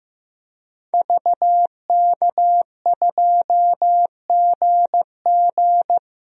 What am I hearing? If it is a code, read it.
Morse code, heard as a keyed tone, about 15 wpm, 700 Hz. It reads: VK2GG
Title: DXCC-100 Call Signs - 5 Characters - 15wpm